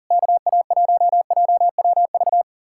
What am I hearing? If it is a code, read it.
Morse code, heard as a keyed tone, about 52 wpm, 700 Hz. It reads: KA1JWV